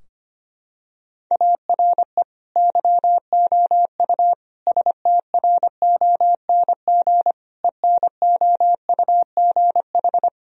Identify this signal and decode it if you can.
Morse code, heard as a keyed tone, about 25 wpm, 700 Hz. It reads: ARE YOU STRONG ENOUGH
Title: Sentences from Top 1000 Words - Review Part 3 - 25wpm